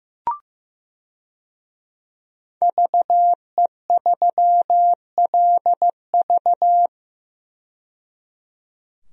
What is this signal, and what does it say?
Morse code, heard as a keyed tone, about 15 wpm, 700 Hz. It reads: VE3LV
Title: DXCC-100 Call Signs - 5 Characters - 15wpm